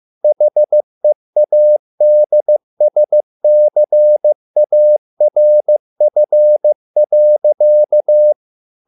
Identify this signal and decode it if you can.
Morse code, heard as a keyed tone, about 15 wpm, 600 Hz. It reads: HEADSCARF.